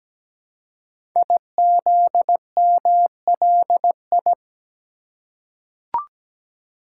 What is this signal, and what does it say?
Morse code, heard as a keyed tone, about 17 wpm, 700 Hz. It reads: IZMLI
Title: Code Groups - 5 Letters - 17wpm